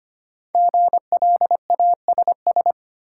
Morse code, heard as a keyed tone, about 25 wpm, 700 Hz. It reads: GLASS
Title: Sentences from Top 1000 Words - Review Part 3 - New Word Drill - 25wpm